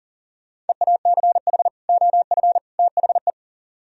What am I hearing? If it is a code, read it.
Morse code, heard as a keyed tone, about 48 wpm, 700 Hz. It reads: EACH OF THE
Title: Speed-Racing - Sets of 3 Words 40wpm